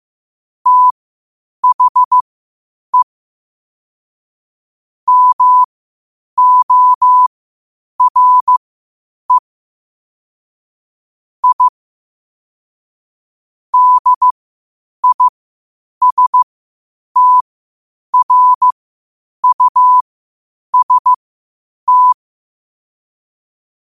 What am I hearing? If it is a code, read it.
Morse code, heard as a keyed tone, about 15 wpm, spaced as for 8 wpm, 1000 Hz. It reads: THE MORE I DISTRUST
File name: qotdfiles / 8wpm